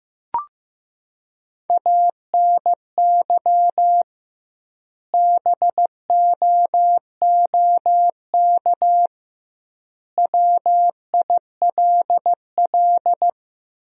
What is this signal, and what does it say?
Morse code, heard as a keyed tone, about 15 wpm, 700 Hz. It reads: ANY BOOK WILL
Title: 2x Word-Spacing - Sentences from Top 300 Words - 15wpm